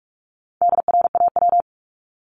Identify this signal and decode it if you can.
Morse code, heard as a keyed tone, about 44 wpm, 700 Hz. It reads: DRAW
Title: ING Suffix - Top 100 Words with Base Words 45wpm